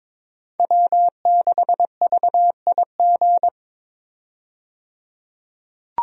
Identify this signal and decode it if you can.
Morse code, heard as a keyed tone, about 22 wpm, 700 Hz. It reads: W6VIG
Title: USA Call Signs - 5 Characters - 22wpm